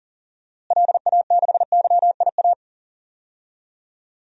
Morse code, heard as a keyed tone, about 40 wpm, 700 Hz. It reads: LA6YIA
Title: DXCC-100 Call Signs - 6 Characters - Repeated 3x - 40wpm